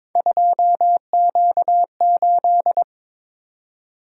Morse code, heard as a keyed tone, about 28 wpm, 700 Hz. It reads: 2Q8